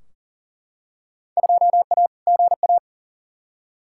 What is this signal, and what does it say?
Morse code, heard as a keyed tone, about 40 wpm, 700 Hz. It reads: 2A GA
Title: Field Day Exchanges - 40wpm